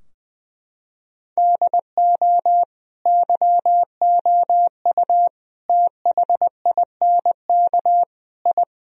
Morse code, heard as a keyed tone, about 21 wpm, 700 Hz. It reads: DO YOU THINK I
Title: Sentences from Top 1000 Words - Review Part 2 - 20wpm